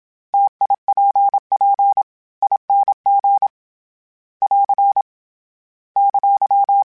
Morse code, heard as a keyed tone, about 27 wpm, 800 Hz. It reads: TIPP ING <AR> !
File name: Variable Speed / Variable Speed-mp3s / T AOEI's sp @25